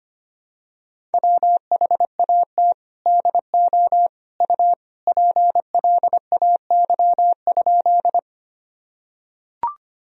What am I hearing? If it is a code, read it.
Morse code, heard as a keyed tone, about 25 wpm, 700 Hz. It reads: WHAT DO U PLAY?
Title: Sentences from Top 200 Words and Abbreviations 25wpm